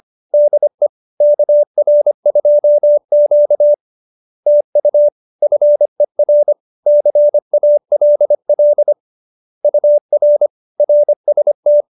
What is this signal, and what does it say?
Morse code, heard as a keyed tone, about 25 wpm, 590 Hz. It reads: DE KR2Q TU FER CALL UR RST
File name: QSO220 25